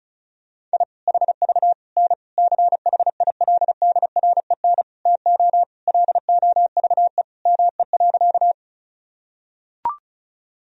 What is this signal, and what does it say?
Morse code, heard as a keyed tone, about 34 wpm, 700 Hz. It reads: I HV N CHILDREN TO LOVE ME.
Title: Sentences from Top 400 Words and Abbreviations 35wpm